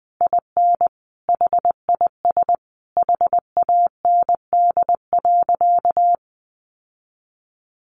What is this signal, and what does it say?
Morse code, heard as a keyed tone, about 20 wpm, 700 Hz. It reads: IN HIS HAND.
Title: Sentences from Top 600 Words 20wpm